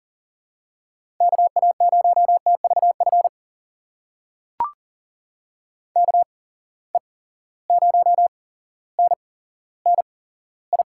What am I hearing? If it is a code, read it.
Morse code, heard as a keyed tone, about 40 wpm, spaced as for 17 wpm, 700 Hz. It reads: KA0TVF KE0NNI
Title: USA Call Signs - 6 Characters - 8x Character Spacing - 40wpm